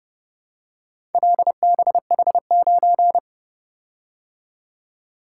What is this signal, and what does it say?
Morse code, heard as a keyed tone, about 29 wpm, 700 Hz. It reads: LBH9